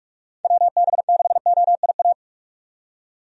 Morse code, heard as a keyed tone, about 42 wpm, 700 Hz. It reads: WB6OIA
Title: USA Call Signs - 6 Characters - 45wpm